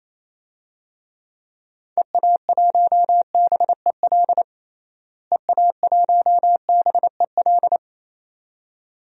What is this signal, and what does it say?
Morse code, heard as a keyed tone, about 28 wpm, 700 Hz. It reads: EA1BEL EA1BEL
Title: DXCC-100 Call Signs - 6 Characters - Repeated 2x - 28wpm